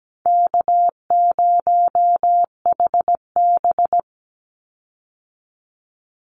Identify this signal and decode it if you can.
Morse code, heard as a keyed tone, about 17 wpm, 700 Hz. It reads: K0HB
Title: Call Signs - 17wpm